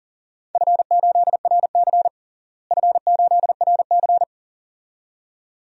Morse code, heard as a keyed tone, about 40 wpm, 700 Hz. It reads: F8RC F8RC
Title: DXCC-100 Call Signs - 4 Characters - Repeated 2x - 40wpm